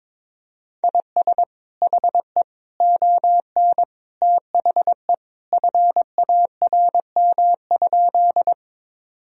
Morse code, heard as a keyed tone, about 22 wpm, 700 Hz. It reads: IS HE ON THE FARM?
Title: Sentences from Top 1000 Words - Review Part 2 - 22wpm